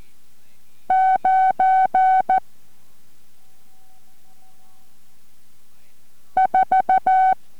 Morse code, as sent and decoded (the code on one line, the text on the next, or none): ----. / ....-
9 4